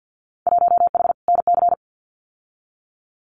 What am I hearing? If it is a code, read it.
Morse code, heard as a keyed tone, about 50 wpm, 700 Hz. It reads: 1H NC
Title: Winter Field Day Exchanges 50wpm